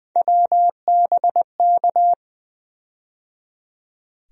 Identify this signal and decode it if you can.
Morse code, heard as a keyed tone, about 20 wpm, 700 Hz. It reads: WBK
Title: USA Call Sign Postfixes - 3 Letters 20wpm